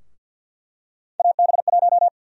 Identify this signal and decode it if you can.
Morse code, heard as a keyed tone, about 51 wpm, 700 Hz. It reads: AB1
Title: USA Call Sign Prefixes - 50wpm